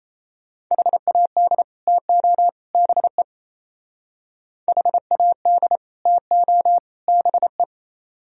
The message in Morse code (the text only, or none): HAD TO BE HAD TO BE